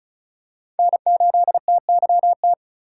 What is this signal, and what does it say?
Morse code, heard as a keyed tone, about 34 wpm, 700 Hz. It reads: N8TYT
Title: USA Call Signs - 5 Characters - 4x Character Spacing - 35wpm